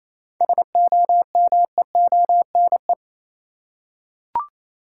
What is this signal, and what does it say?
Morse code, heard as a keyed tone, about 28 wpm, 700 Hz. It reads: SOMEONE
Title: Seven Letter Words - 28wpm